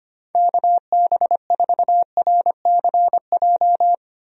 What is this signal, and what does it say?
Morse code, heard as a keyed tone, about 25 wpm, 700 Hz. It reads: KB4RCJ
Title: USA Call Signs - 6 Characters - 25wpm